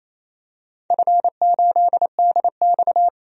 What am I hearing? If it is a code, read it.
Morse code, heard as a keyed tone, about 28 wpm, 700 Hz. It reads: F8DX
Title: DXCC-100 Call Signs - 4 Characters - 28wpm